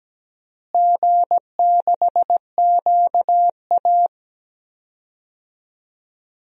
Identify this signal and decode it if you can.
Morse code, heard as a keyed tone, about 17 wpm, 700 Hz. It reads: G6QA
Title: DXCC-100 Call Signs - 4 Characters - Repeated 3x - 17wpm